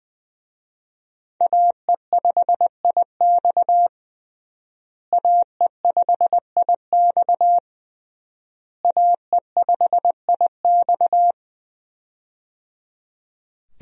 Morse code, heard as a keyed tone, about 20 wpm, 700 Hz. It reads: AE5IX AE5IX AE5IX
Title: USA Call Signs - 5 Characters - Repeated 3x - 20wpm